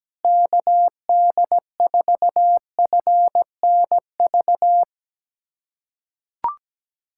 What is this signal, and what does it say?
Morse code, heard as a keyed tone, about 17 wpm, 700 Hz. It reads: KD4FNV
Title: USA Callsign DE Callsign 17wpm